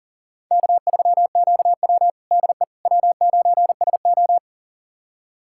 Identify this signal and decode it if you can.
Morse code, heard as a keyed tone, about 37 wpm, 700 Hz. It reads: K3QW DE W9SO
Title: USA Callsign DE Callsign - 4 Characters - 40wpm